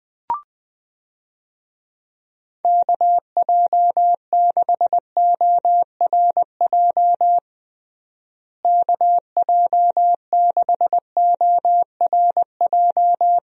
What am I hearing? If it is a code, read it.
Morse code, heard as a keyed tone, about 20 wpm, 700 Hz. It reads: KJ6ORJ KJ6ORJ
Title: USA Call Signs - 6 Characters - Repeated 2x - 20wpm